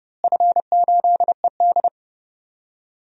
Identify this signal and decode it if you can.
Morse code, heard as a keyed tone, about 30 wpm, 700 Hz. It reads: F8ED
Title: DXCC-100 Call Signs - 4 Characters - Repeated 3x - 30wpm